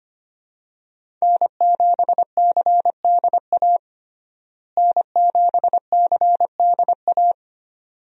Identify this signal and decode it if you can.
Morse code, heard as a keyed tone, about 25 wpm, 700 Hz. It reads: N7CDA N7CDA
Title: USA Call Signs - 5 Characters - Repeated 2x - 25wpm